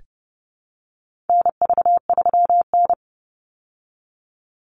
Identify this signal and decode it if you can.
Morse code, heard as a keyed tone, about 30 wpm, 700 Hz. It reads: NV3N
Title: Call Signs Repeated 2x - 30wpm